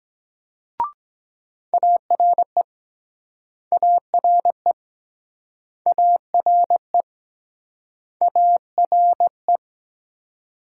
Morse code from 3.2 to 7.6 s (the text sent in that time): ARE ARE